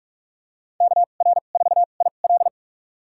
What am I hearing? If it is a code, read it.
Morse code, heard as a keyed tone, about 42 wpm, spaced as for 31 wpm, 700 Hz. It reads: KR4IL
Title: USA Call Signs - 5 Characters - 2x Character Spacing - 45wpm